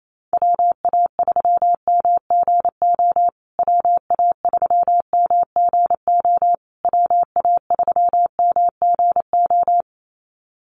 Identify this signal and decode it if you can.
Morse code, heard as a keyed tone, about 28 wpm, 700 Hz. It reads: WA3MGO WA3MGO WA3MGO
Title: USA Call Signs Repeated 3x 28wpm